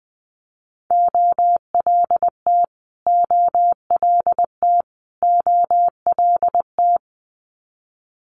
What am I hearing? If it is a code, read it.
Morse code, heard as a keyed tone, about 20 wpm, 700 Hz. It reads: OLT OLT OLT
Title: Multi-letter Phonemes Repeated 3x 20wpm